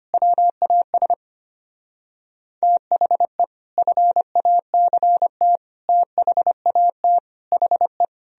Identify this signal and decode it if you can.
Morse code, heard as a keyed tone, about 32 wpm, 700 Hz. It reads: WAS THE FACT THAT HE